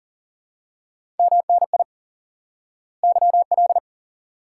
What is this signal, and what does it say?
Morse code, heard as a keyed tone, about 38 wpm, 700 Hz. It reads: MNI YL
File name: course / instant-qso-element-course-3 / IQE Course 3 - Lesson 89 - YAGI - Two Elements - 40wpm